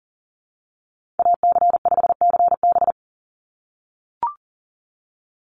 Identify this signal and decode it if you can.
Morse code, heard as a keyed tone, about 40 wpm, 690 Hz. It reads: AC5CB
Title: USA Call Signs 40wpm